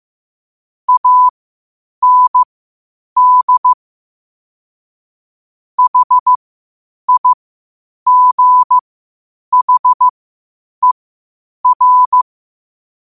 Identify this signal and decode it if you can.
Morse code, heard as a keyed tone, about 15 wpm, spaced as for 8 wpm, 1000 Hz. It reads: AND HIGHER
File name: qotdfiles / 8wpm